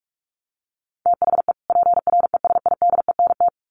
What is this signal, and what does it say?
Morse code, heard as a keyed tone, about 44 wpm, 700 Hz. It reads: THE PRESIDENT
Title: Sets of 2 Words 45wpm